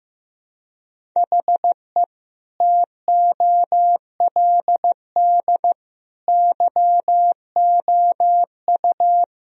Morse code, heard as a keyed tone, about 15 wpm, 700 Hz. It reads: HE TOLD YOU
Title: Sentences from Top 1000 Words - Review Part 2 - 15wpm